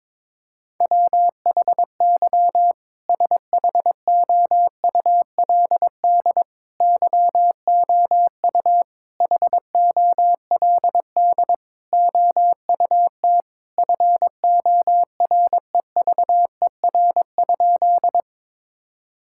Morse code, heard as a keyed tone, about 22 wpm, 700 Hz. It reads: WHY SHOULD YOU HOLD OUT FOREVER?
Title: Sentences from Top 1600 Words 22wpm